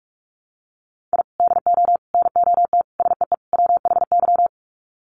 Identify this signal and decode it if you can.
Morse code, heard as a keyed tone, about 44 wpm, 700 Hz. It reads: I DO NOT SEE WHY
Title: Sets of 5 Words Encore 45wpm